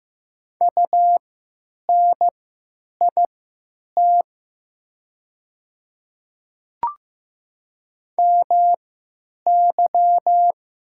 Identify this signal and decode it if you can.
Morse code, heard as a keyed tone, about 15 wpm, spaced as for 9 wpm, 700 Hz. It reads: UNIT MY
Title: 3x Character Spacing - Top 500 Words - 15wpm